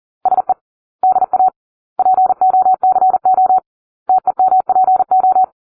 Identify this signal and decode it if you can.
Morse code, heard as a keyed tone, about 40 wpm, 750 Hz. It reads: SE DA POCO TEMPO